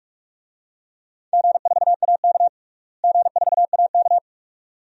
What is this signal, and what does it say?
Morse code, heard as a keyed tone, about 46 wpm, 700 Hz. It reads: G4AK G4AK
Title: DXCC-100 Call Signs - 4 Characters - Repeated 2x - 45wpm